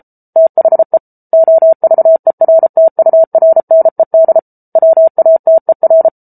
Morse code, heard as a keyed tone, about 34 wpm, 650 Hz. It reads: THE OVERTURNED WATER